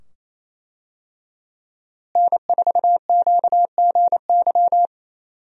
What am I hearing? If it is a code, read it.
Morse code, heard as a keyed tone, about 28 wpm, 700 Hz. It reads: N4QGY
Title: USA Call Signs - 5 Characters - Repeated 2x - 28wpm